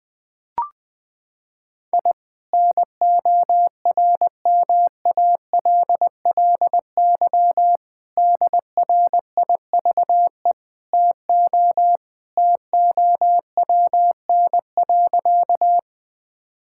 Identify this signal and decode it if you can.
Morse code, heard as a keyed tone, about 20 wpm, 700 Hz. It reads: I NORMALLY DRIVE TO TOWN.
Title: Sentences from Top 1000 Words - Review Part 6 - 20wpm